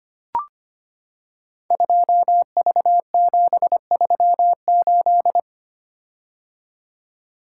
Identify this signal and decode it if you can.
Morse code, heard as a keyed tone, about 25 wpm, 700 Hz. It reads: 2V738
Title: Code Groups - 5 Letter:Numbers - 25wpm